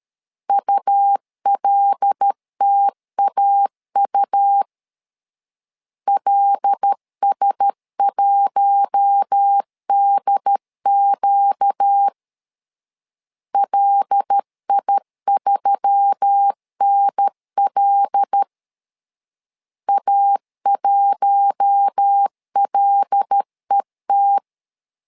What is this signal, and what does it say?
Morse code, heard as a keyed tone, about 13 wpm, 800 Hz. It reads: ULTAU LS1DQ LI3NL A1LET
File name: LEZIONI / 5 LBVC / 5 L 12wpm 1913